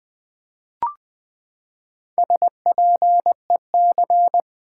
Morse code, heard as a keyed tone, about 20 wpm, 700 Hz. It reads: SPEC
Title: Build A Word - 4 Letters At A Time - 11-Letter Words - 20wpm